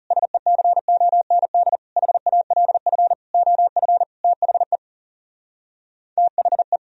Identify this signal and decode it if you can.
Morse code, heard as a keyed tone, about 40 wpm, 700 Hz. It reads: SECOND HALF OF THE THE